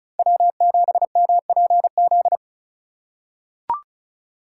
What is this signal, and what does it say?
Morse code, heard as a keyed tone, about 36 wpm, 700 Hz. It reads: W7MPZ
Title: USA Call Signs - 5 Characters - 4x Character Spacing - 35wpm